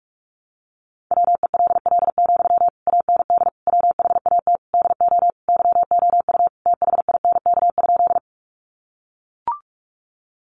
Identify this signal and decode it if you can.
Morse code, heard as a keyed tone, about 44 wpm, 700 Hz. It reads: WELL, AND WHAT DO YOU THINK?
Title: Sentences from Top 200 Words 45wpm